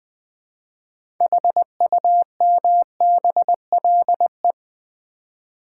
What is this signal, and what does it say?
Morse code, heard as a keyed tone, about 20 wpm, 700 Hz. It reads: HUMBLE
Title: 6 Letter Words - First 3 Letters Given - 20wpm